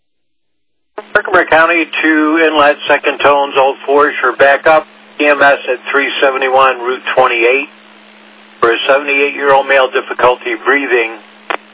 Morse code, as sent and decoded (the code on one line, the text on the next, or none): none